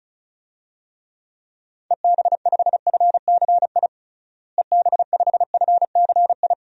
Morse code, heard as a keyed tone, about 36 wpm, 700 Hz. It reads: EB5FCI EB5FCI
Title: DXCC-100 Call Signs - 6 Characters - Repeated 3x - 35wpm